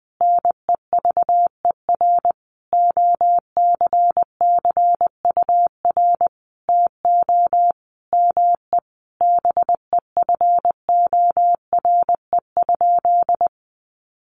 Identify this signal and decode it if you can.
Morse code, heard as a keyed tone, about 20 wpm, 700 Hz. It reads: NEVER OCCUR TO ME BEFORE?
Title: Sentences from Top 900 Words 20wpm